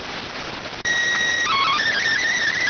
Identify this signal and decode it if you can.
SSTV (slow-scan television), PD 290.